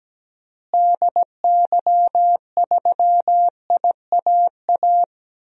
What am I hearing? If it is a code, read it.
Morse code, heard as a keyed tone, about 17 wpm, 700 Hz. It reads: DY3IAA
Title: DXCC-100 Call Signs - 6 Characters - Repeated 3x - 17wpm